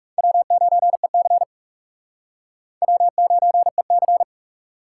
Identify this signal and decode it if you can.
Morse code, heard as a keyed tone, about 38 wpm, 700 Hz. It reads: W9EC W9EC